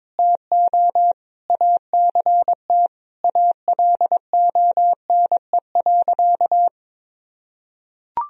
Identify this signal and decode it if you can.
Morse code, heard as a keyed tone, about 22 wpm, 700 Hz. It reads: TO ACT ALONE.
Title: Sentences from Top 1000 Words - Review Part 1 - 22wpm